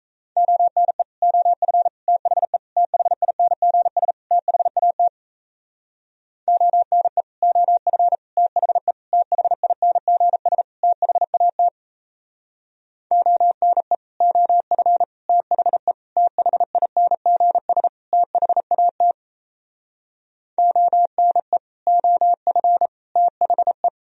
Morse code, heard as a keyed tone, about 37 wpm, 700 Hz. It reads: ONE OF THE THINGS THAT ONE OF THE THINGS THAT ONE OF THE THINGS THAT ONE OF THE